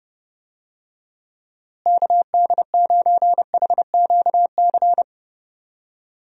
Morse code, heard as a keyed tone, about 30 wpm, 700 Hz. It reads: KD9HQC
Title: USA Call Signs - 6 Characters - Repeated 3x - 30wpm